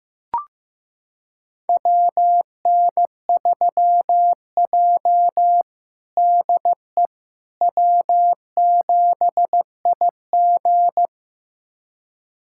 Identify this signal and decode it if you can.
Morse code, heard as a keyed tone, about 15 wpm, 700 Hz. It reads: WN3J DE W7IG
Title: USA Callsign DE Callsign - 4 Characters - 15wpm